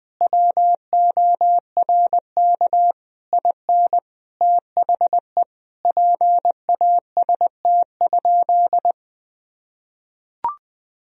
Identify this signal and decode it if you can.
Morse code, heard as a keyed tone, about 20 wpm, 700 Hz. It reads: WORK IN THE PAST?
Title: Sentences from Top 1000 Words - Review Part 4 - 20wpm